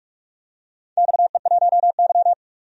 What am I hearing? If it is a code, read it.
Morse code, heard as a keyed tone, about 42 wpm, 700 Hz. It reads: XE1Y
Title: DXCC-100 Call Signs - 4 Characters - 45wpm